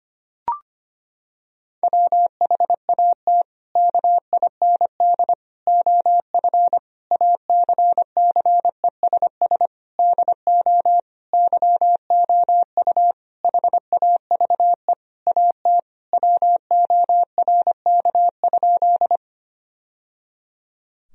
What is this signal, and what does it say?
Morse code, heard as a keyed tone, about 25 wpm, 700 Hz. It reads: WHAT KIND OF ACCESS DO YOU HAVE AT WORK?
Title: Sentences from Top 1000 Words - Review Part 4 - 25wpm